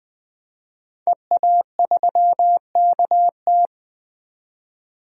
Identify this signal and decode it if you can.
Morse code, heard as a keyed tone, about 20 wpm, 700 Hz. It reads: EA3KT
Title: DXCC-100 Call Signs - 5 Characters - 20wpm